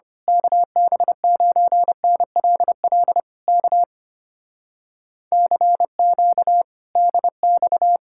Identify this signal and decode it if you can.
Morse code, heard as a keyed tone, about 24 wpm, 700 Hz. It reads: KB9NLL K CQ DX
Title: Speed-Racing Common QSO Phrases - 25wpm